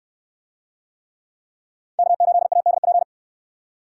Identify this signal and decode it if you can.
Morse code, heard as a keyed tone, about 61 wpm, 700 Hz. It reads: K8ADP